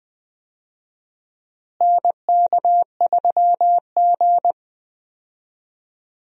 Morse code, heard as a keyed tone, about 20 wpm, 700 Hz. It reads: NK3G
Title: USA Call Signs - 4 Characters - Repeated 2x - 20wpm